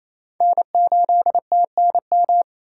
Morse code, heard as a keyed tone, about 34 wpm, 700 Hz. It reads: N8TNM
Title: Speed Racing - USA 1x3 Call Signs 28wpm